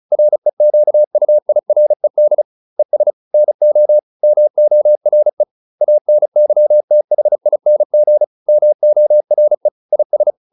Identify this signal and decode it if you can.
Morse code, heard as a keyed tone, about 34 wpm, 600 Hz. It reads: REQUIRED ES NO MORE ANYTHING MORE IS